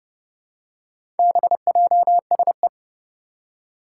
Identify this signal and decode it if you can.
Morse code, heard as a keyed tone, about 30 wpm, 700 Hz. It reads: BJSE
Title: Four Challenging Letter-Number Combinations 30wpm